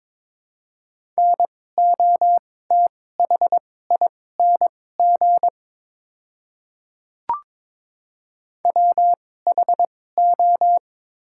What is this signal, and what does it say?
Morse code, heard as a keyed tone, about 21 wpm, spaced as for 16 wpm, 700 Hz. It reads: NOTHING WHO
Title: 2x Character Spacing - Top 500 Words - 22wpm